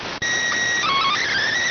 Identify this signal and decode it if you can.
SSTV (slow-scan television), PD 290.